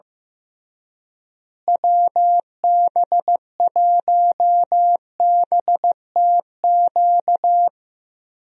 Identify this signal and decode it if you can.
Morse code, heard as a keyed tone, about 15 wpm, 700 Hz. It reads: WB1BTQ
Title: USA Call Signs - 6 Characters - Repeated 3x - 15wpm